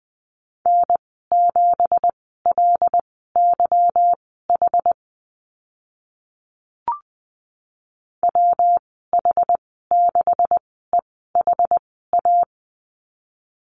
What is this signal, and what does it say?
Morse code, heard as a keyed tone, about 19 wpm, spaced as for 14 wpm, 700 Hz. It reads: N7LYH WH6EHA
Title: 179 Lesson 29g - Slash - Call Signs 20wpm